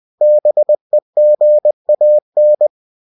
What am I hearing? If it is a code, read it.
Morse code, heard as a keyed tone, about 20 wpm, 600 Hz. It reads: BEGAN
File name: cw-books / winnie-the-pooh / winnie-20wpm-page-0103